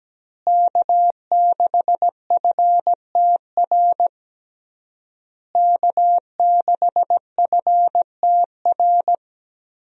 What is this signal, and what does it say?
Morse code, heard as a keyed tone, about 17 wpm, 700 Hz. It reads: K6FTR K6FTR
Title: USA Call Signs - 5 Characters - Repeated 3x - 17wpm